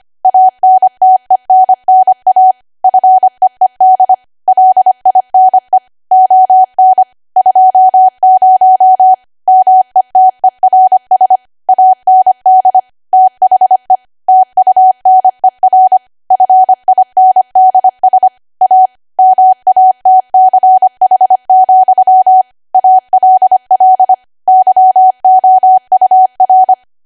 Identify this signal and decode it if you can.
Morse code, heard as a keyed tone, about 25 wpm, 750 Hz. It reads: ANTENNA FEED LINE ON 20 METERS AND THE TUNER FINDS A MATCH, ALL YOUR
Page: 25 WPM Code Archive